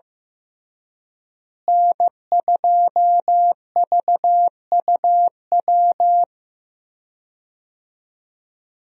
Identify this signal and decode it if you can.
Morse code, heard as a keyed tone, about 15 wpm, 700 Hz. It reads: N2VUW
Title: USA Call Signs - 5 Characters - Repeated 3x - 15wpm